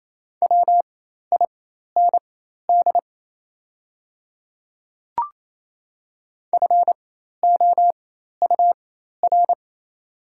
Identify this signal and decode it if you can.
Morse code, heard as a keyed tone, about 28 wpm, spaced as for 13 wpm, 700 Hz. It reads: WIND FOUR
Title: 4x Character Spacing - Top 500 Words - 28wpm